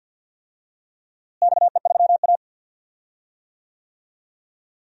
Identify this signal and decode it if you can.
Morse code, heard as a keyed tone, about 51 wpm, 700 Hz. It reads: XE3A
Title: DXCC-100 Call Signs - 4 Characters - Repeated 2x - 50wpm